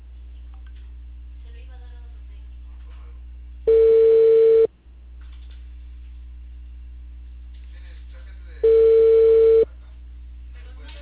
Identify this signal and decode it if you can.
Ringback tone (China, 450 Hz), with speech.